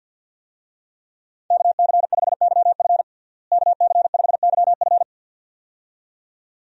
Morse code, heard as a keyed tone, about 47 wpm, 700 Hz. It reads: KC5YF KC5YF
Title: USA Call Signs - 5 Characters - Repeated 2x - 50wpm